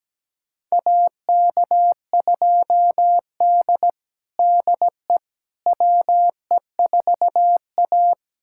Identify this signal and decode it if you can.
Morse code, heard as a keyed tone, about 17 wpm, 700 Hz. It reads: AK2D DE WE4A
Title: USA Callsign DE Callsign - 4 Characters - 17wpm